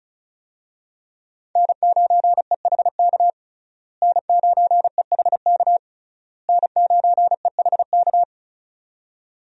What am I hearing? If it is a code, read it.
Morse code, heard as a keyed tone, about 35 wpm, 700 Hz. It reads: N9EHK N9EHK N9EHK
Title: USA Call Signs - 5 Characters - Repeated 3x - 35wpm